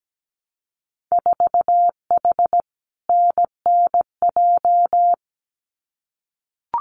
Morse code, heard as a keyed tone, about 17 wpm, 700 Hz. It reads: 4H NNJ
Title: Winter Field Day Exchanges Encore 17wpm